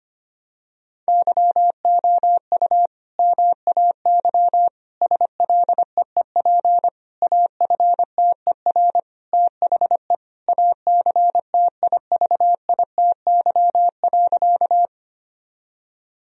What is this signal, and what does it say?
Morse code, heard as a keyed tone, about 25 wpm, 700 Hz. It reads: YOU MAY SLEEP AFTER THE ACTIVITY.
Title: Sentences from Top 1000 Words - Review Part 4 - 25wpm